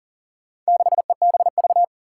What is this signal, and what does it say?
Morse code, heard as a keyed tone, about 38 wpm, 700 Hz. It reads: 6EBV
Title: Four Challenging Letter-Number Combinations 40wpm